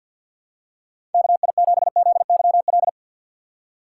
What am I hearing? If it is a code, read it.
Morse code, heard as a keyed tone, about 52 wpm, 700 Hz. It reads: KI7ZYL
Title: USA Call Signs - 3x Character Spacing - 50wpm